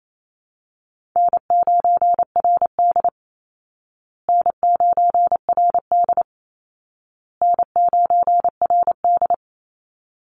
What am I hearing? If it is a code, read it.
Morse code, heard as a keyed tone, about 28 wpm, 700 Hz. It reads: N9RD N9RD N9RD
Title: Call Signs Repeated 2x - 28wpm